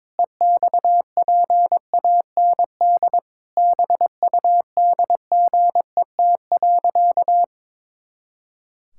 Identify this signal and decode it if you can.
Morse code, heard as a keyed tone, about 22 wpm, 700 Hz. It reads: EXPAND BUDGET.
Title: Pangram Challenge v2 - 22wpm